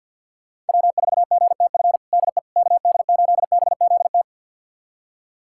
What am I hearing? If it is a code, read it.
Morse code, heard as a keyed tone, about 46 wpm, 700 Hz. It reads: W4GTF DE KD8BZT
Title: USA Callsign DE Callsign 50wpm